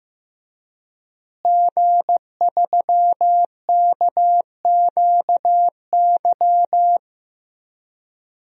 Morse code, heard as a keyed tone, about 15 wpm, 700 Hz. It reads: G3KQY
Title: DXCC-100 Call Signs - 5 Characters - Repeated 3x - 15wpm